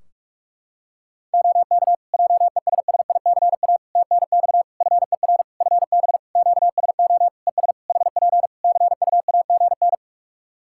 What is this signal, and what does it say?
Morse code, heard as a keyed tone, about 47 wpm, 700 Hz. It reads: OK JESSICA TNX FER FB QSO ES HP CUAGN
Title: Practice QSO - Exchange 4 - Ending - 45wpm